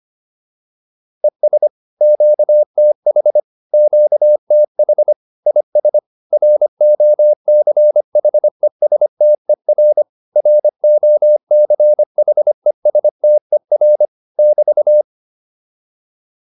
Morse code, heard as a keyed tone, about 25 wpm, 600 Hz. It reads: ES QTH QTH IS ROCHESTER ROCHESTER <BT>